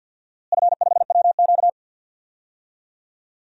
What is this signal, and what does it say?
Morse code, heard as a keyed tone, about 50 wpm, 700 Hz. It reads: F5WQ
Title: DXCC-100 Call Signs - 4 Characters - Repeated 2x - 50wpm